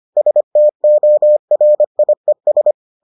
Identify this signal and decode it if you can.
Morse code, heard as a keyed tone, about 25 wpm, 600 Hz. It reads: STORIES